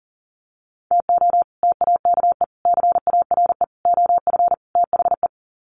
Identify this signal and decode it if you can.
Morse code, heard as a keyed tone, about 40 wpm, 700 Hz. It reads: TO TAKE CARE OF THE
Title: Sets of 5 Words Encore 40wpm